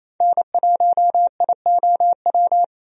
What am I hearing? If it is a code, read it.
Morse code, heard as a keyed tone, about 28 wpm, 700 Hz. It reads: N1IOW
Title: USA Call Signs - 5 Characters - 28wpm